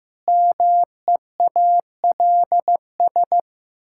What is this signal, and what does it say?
Morse code, heard as a keyed tone, about 15 wpm, 700 Hz. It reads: MEALS
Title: Pangram Challenge v2 - 15wpm